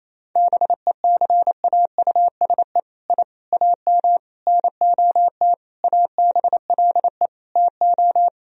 Morse code, heard as a keyed tone, about 28 wpm, 700 Hz. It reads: BECAUSE I AM NOT ABLE TO
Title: Sentences from Top 1000 Words - Review Part 3 - 28wpm